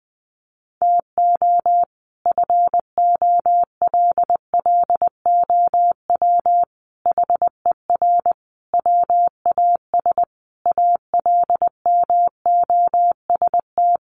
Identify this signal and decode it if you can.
Morse code, heard as a keyed tone, about 20 wpm, 700 Hz. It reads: TO FOLLOW HER WAS ALMOST
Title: Sentences from Top 1100 Words 20wpm